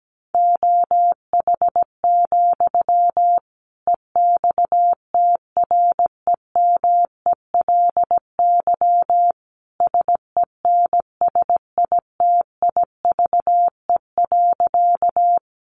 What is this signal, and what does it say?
Morse code, heard as a keyed tone, about 17 wpm, 700 Hz. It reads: OH, EXTREMELY SENSITIVE.
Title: Sentences from Top 1400 Words 17wpm